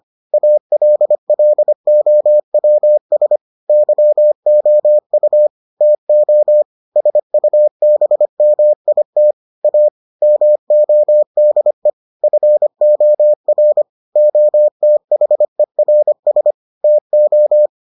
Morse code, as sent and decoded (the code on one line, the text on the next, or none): .- .-.. .-.. --- .-- ... / -.-- --- ..- / - --- / ... ..- -... -- .. - / .- / -- --- -.. / ..-. --- .-. / --- - .... .-. ... / - ---
ALLOWS YOU TO SUBMIT A MODE FOR OTHERS TO